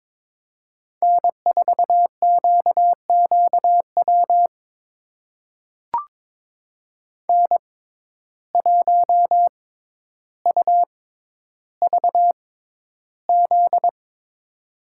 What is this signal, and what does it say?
Morse code, heard as a keyed tone, about 22 wpm, spaced as for 11 wpm, 700 Hz. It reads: N4QQW N1UVZ
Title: USA Call Signs - 5 Characters - 6x Character Spacing - 22wpm